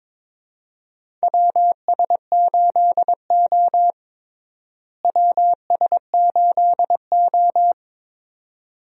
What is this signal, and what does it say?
Morse code, heard as a keyed tone, about 21 wpm, 700 Hz. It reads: WS8O WS8O
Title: USA Call Signs - 4 Characters - Repeated 2x - 22wpm